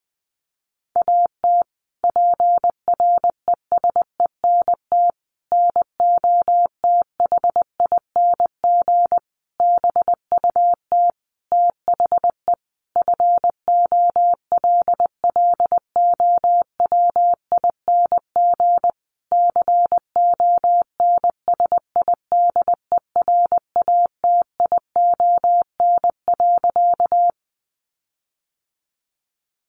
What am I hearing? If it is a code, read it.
Morse code, heard as a keyed tone, about 20 wpm, 700 Hz. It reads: AT PRESENT NOTHING BUT THE FOLLOWING CONSIDERATION.
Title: Sentences from Top 1500 Words 20wpm